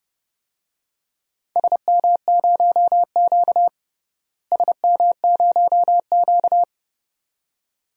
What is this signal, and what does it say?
Morse code, heard as a keyed tone, about 31 wpm, 700 Hz. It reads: SM0Q SM0Q
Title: DXCC-100 Call Signs - 4 Characters - Repeated 2x - 30wpm